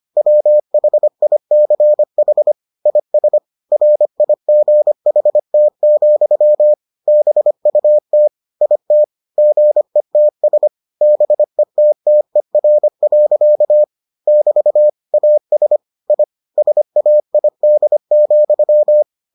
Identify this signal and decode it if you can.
Morse code, heard as a keyed tone, about 25 wpm, 600 Hz. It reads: WHICH IS RIGHT, BUT IT GETS BETTER. <BT> AS I SAID,